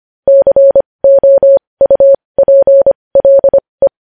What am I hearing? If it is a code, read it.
Morse code, heard as a keyed tone, about 25 wpm, spaced as for 20 wpm, 550 Hz. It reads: COUPLE
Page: News headlines in Morse code at 20 WPM